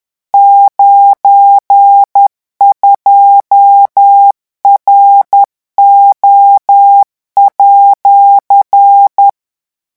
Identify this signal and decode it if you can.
Morse code, heard as a keyed tone, about 10 wpm, 800 Hz. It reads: 92RO@
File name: morse mp3 / revision19-vitesse 10 mots